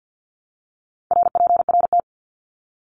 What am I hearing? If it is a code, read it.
Morse code, heard as a keyed tone, about 50 wpm, 700 Hz. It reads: RPRT
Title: Abbreviations Repeated 3x - 50wpm